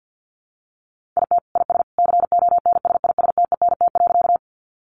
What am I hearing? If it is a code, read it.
Morse code, heard as a keyed tone, about 50 wpm, 700 Hz. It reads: IT IS CONSISTENT.
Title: Sentences from Top 1500 Words 50wpm